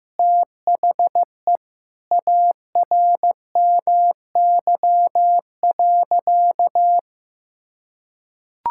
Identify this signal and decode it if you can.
Morse code, heard as a keyed tone, about 15 wpm, 700 Hz. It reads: THE ARMY.
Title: Sentences from Top 1000 Words - Review Part 5 - 15wpm